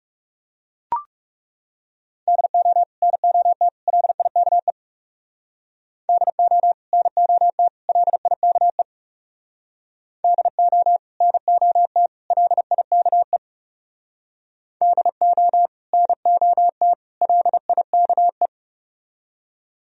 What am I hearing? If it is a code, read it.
Morse code, heard as a keyed tone, about 40 wpm, 700 Hz. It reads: DO NOT LIKE DO NOT LIKE DO NOT LIKE DO NOT LIKE